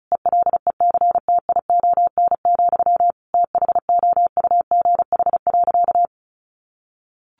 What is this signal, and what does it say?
Morse code, heard as a keyed tone, about 34 wpm, 700 Hz. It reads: ELECTION, THOUGH.
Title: Sentences from Top 1500 Words 35wpm